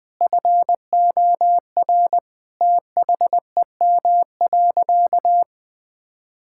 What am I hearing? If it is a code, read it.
Morse code, heard as a keyed tone, about 21 wpm, 700 Hz. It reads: FOR THEM.
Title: Sentences from Top 1000 Words - Review Part 5 - 20wpm